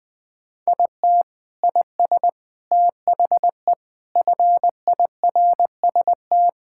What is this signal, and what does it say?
Morse code, heard as a keyed tone, about 21 wpm, 700 Hz. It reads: IT IS THE FIRST
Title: Sentences with 4-Word Phrases - Spoken Beginning - 20wpm